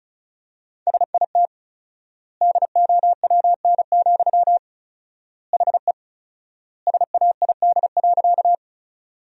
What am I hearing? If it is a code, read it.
Morse code, heard as a keyed tone, about 36 wpm, 700 Hz. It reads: SIT DOWN, HE SAID.
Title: 4x Word-Spacing - Sentences from Top 500 Words - 35wpm